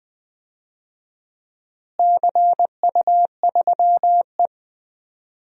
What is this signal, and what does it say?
Morse code, heard as a keyed tone, about 20 wpm, 700 Hz. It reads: CU3E
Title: DXCC-100 Call Signs - 4 Characters - Repeated 3x - 20wpm